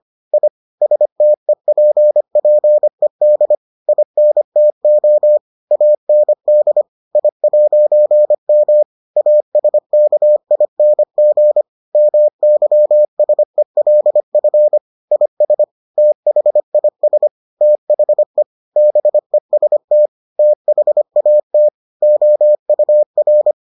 Morse code, heard as a keyed tone, about 25 wpm, 600 Hz. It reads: I STEPPED INTO AND I'M ASKING MYSELF IS THIS THE BEST THAT OUR